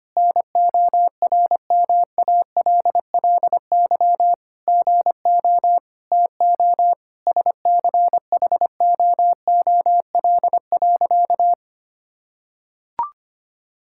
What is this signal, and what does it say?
Morse code, heard as a keyed tone, about 25 wpm, 700 Hz. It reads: NORMALLY GO TO SCHOOL.
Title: Sentences from Top 1000 Words - Review Part 6 - 25wpm